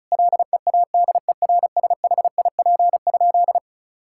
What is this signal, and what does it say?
Morse code, heard as a keyed tone, about 36 wpm, 700 Hz. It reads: LEADERSHIP?
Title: Sentences from Top 1600 Words 35wpm